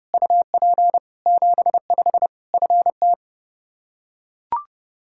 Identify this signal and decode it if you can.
Morse code, heard as a keyed tone, about 30 wpm, 700 Hz. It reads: UP 75 FT
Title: Practice QSO - Exchange 2 - Info - 30wpm